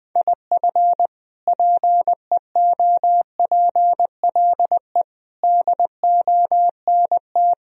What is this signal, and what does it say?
Morse code, heard as a keyed tone, about 21 wpm, 700 Hz. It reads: IF PEOPLE DONT
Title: Sentences from Top 1000 Words - Review Part 3 - 20wpm